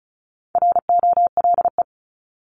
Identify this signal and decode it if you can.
Morse code, heard as a keyed tone, about 34 wpm, 700 Hz. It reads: ROLE
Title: Top 751-1000 Words 35wpm